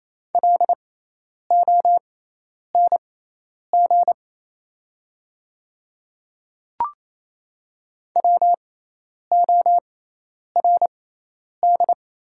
Morse code, heard as a keyed tone, about 28 wpm, spaced as for 10 wpm, 690 Hz. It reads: LONG WORD
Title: 6x Character Spacing - Top 300 Words - 28wpm